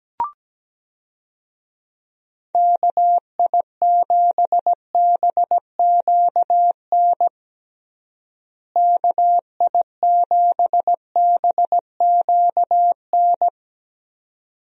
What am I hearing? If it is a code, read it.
Morse code, heard as a keyed tone, about 17 wpm, 700 Hz. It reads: KI7BQN KI7BQN
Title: USA Call Signs - 6 Characters - Repeated 3x - 17wpm